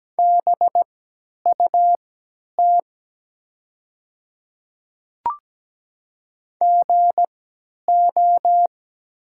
Morse code, heard as a keyed tone, about 17 wpm, spaced as for 10 wpm, 690 Hz. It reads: BUT GO
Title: 3x Character Spacing - Top 100 Words - 17wpm